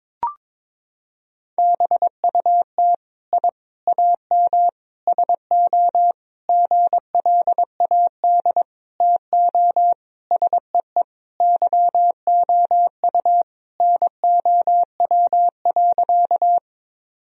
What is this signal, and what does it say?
Morse code, heard as a keyed tone, about 22 wpm, 700 Hz. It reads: BUT I AM SO GLAD TO SEE YOU NOW.
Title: Sentences from Top 1000 Words - Review Part 2 - 22wpm